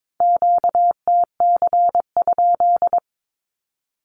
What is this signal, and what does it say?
Morse code, heard as a keyed tone, about 22 wpm, 700 Hz. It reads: QTC?
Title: Q-Codes with Spoken Meaning - 22wpm